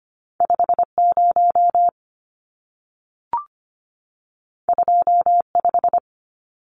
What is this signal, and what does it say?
Morse code, heard as a keyed tone, about 25 wpm, 700 Hz. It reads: 50 25
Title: Numbers - 2-Digit 25wpm